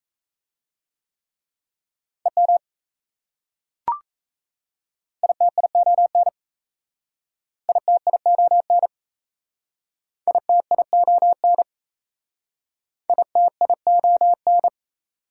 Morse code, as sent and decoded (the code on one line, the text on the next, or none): . -- / .. - .. --- -. / .. - .. --- -. / .. - .. --- -. / .. - .. --- -.
EM ITION ITION ITION ITION